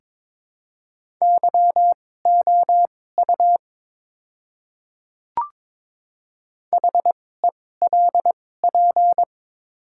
Morse code, heard as a keyed tone, about 23 wpm, spaced as for 16 wpm, 700 Hz. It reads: YOU HELP
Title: 2x Character Spacing - Top 200 Words - 22wpm